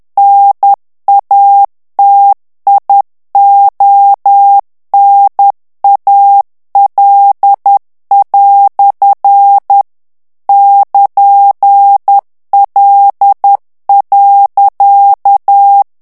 Morse code, heard as a keyed tone, about 10 wpm, 800 Hz. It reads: NATIONAL" <KN>L.
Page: Français - texte en clair 1 - vitesse : 10 mots minute